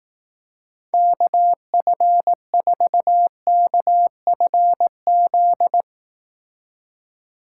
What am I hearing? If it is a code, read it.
Morse code, heard as a keyed tone, about 18 wpm, 700 Hz. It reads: KF4KFZ
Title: Speed Racing - USA 2x3 Call Signs 15wpm